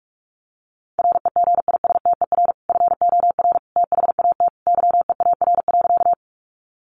Morse code, heard as a keyed tone, about 44 wpm, 700 Hz. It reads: REGISTER FOR THAT YEAR.
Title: Sentences from Top 1200 Words 45wpm